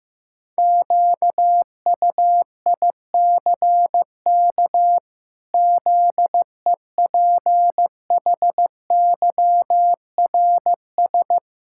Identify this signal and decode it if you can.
Morse code, heard as a keyed tone, about 15 wpm, 700 Hz. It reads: QUICK ZEPHYRS
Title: Pangram Challenge v2 - 15wpm